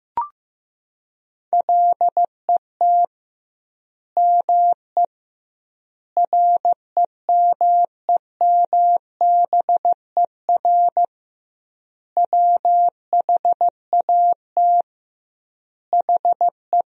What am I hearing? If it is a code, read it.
Morse code, heard as a keyed tone, about 15 wpm, 700 Hz. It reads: LET ME REMEMBER WHAT HE
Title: 2x Word-Spacing - Sentences from Top 400 Words - 15wpm